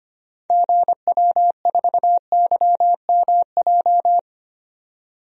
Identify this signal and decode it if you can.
Morse code, heard as a keyed tone, about 25 wpm, 700 Hz. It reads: GW4YMJ
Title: DXCC-100 Call Signs - 6 Characters - 25wpm